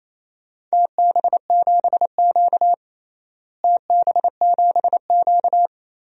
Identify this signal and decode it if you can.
Morse code, heard as a keyed tone, about 28 wpm, 700 Hz. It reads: TB7Q TB7Q
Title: DXCC-100 Call Signs - 4 Characters - Repeated 2x - 28wpm